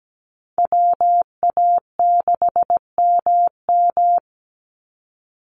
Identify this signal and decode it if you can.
Morse code, heard as a keyed tone, about 17 wpm, 700 Hz. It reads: WA6MM
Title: NA SOTA Activator Callsigns 17wpm